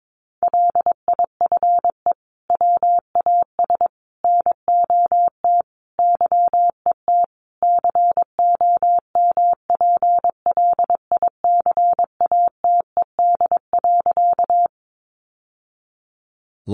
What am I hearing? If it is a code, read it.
Morse code, heard as a keyed tone, about 22 wpm, 700 Hz. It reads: LIFE WAS NOT YET COMPLICATED.
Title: Sentences from Top 1200 Words 22wpm